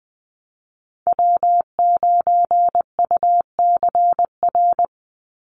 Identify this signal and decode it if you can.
Morse code, heard as a keyed tone, about 19 wpm, 700 Hz. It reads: W9UCR
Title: Call Signs - 20wpm